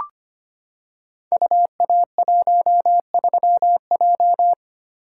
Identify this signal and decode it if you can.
Morse code, heard as a keyed tone, about 25 wpm, 700 Hz. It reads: UA13J
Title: Code Groups - 5 Letter:Numbers - 25wpm